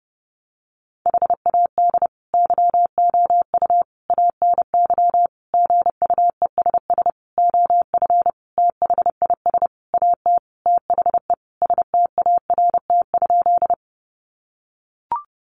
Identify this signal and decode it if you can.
Morse code, heard as a keyed tone, about 30 wpm, 700 Hz. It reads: HAD YOU ANY GUESS OF THIS AT THE START?
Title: Sentences from Top 1500 Words 30wpm